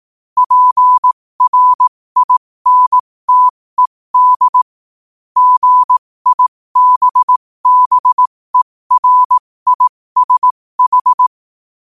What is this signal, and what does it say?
Morse code, heard as a keyed tone, about 18 wpm, spaced as for 15 wpm, 1000 Hz. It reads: PRINTED GIBBERISH
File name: qotdfiles / 15wpm